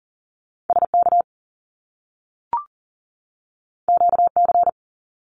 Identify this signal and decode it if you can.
Morse code, heard as a keyed tone, about 40 wpm, 700 Hz. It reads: SK QC
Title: Canadian Provinces and Territories 40wpm